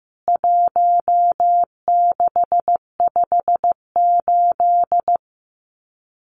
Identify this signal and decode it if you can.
Morse code, heard as a keyed tone, about 15 wpm, 700 Hz. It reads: 1658
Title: CWT Exchange 15wpm